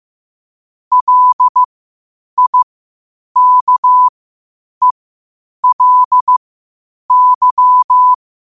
Morse code, heard as a keyed tone, about 15 wpm, spaced as for 8 wpm, 1000 Hz. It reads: LIKELY
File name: qotdfiles / 8wpm